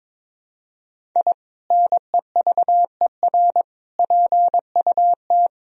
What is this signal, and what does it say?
Morse code, heard as a keyed tone, about 22 wpm, 700 Hz. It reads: I NEVER PUT
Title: Sentences from Top 1000 Words - Review Part 1 - 22wpm